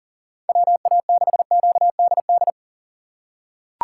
Morse code, heard as a keyed tone, about 37 wpm, 690 Hz. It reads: WA6QDD
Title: USA Call Signs - 6 Characters - 2x Character Spacing - 40wpm